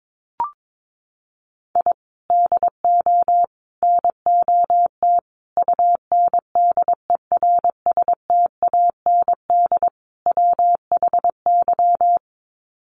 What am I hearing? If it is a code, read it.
Morse code, heard as a keyed tone, about 22 wpm, 700 Hz. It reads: I DO NOT UNDERSTAND WHY
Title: Sets of 5 Words Encore 22wpm